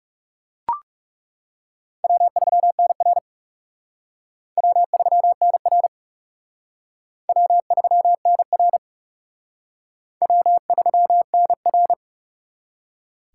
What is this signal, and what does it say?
Morse code, heard as a keyed tone, about 36 wpm, 700 Hz. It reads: W3NR W3NR W3NR W3NR
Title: Speed Racing - USA 1x2 Call Signs 30wpm